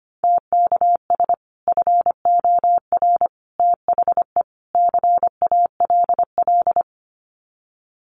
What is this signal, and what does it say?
Morse code, heard as a keyed tone, about 25 wpm, 700 Hz. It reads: TKS FOR THE CALL
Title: Common QSO Phrases 25wpm